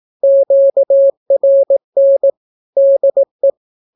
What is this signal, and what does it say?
Morse code, heard as a keyed tone, about 18 wpm, 550 Hz. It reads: QRN DE
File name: naukapodstaw / skroty / skroty